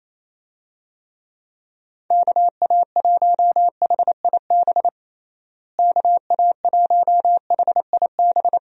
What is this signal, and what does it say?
Morse code, heard as a keyed tone, about 28 wpm, 700 Hz. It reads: KA1HIB KA1HIB
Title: USA Call Signs - 6 Characters - Repeated 2x - 28wpm